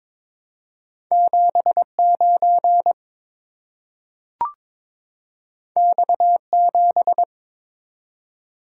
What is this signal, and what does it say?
Morse code, heard as a keyed tone, about 22 wpm, 700 Hz. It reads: 79 X7
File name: practice / 22wpm / Two Letter-Number Combinations - 22wpm